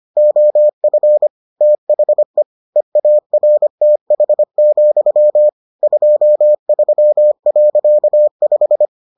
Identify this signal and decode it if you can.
Morse code, heard as a keyed tone, about 25 wpm, 600 Hz. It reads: OF THE EARTH, 23.5